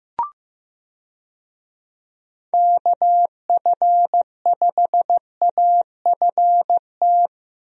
Morse code, heard as a keyed tone, about 15 wpm, 700 Hz. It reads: KF5AFT
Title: USA Call Signs - 6 Characters - Repeated 3x - 15wpm